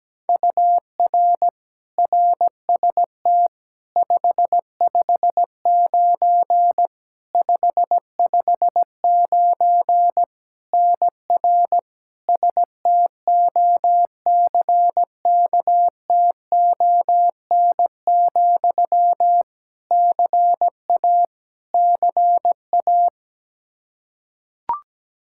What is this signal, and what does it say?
Morse code, heard as a keyed tone, about 17 wpm, 700 Hz. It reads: UR RST 559 559 NR STOCKTON, CA CA
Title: Practice QSO - Exchange 1 - Beginning - 17wpm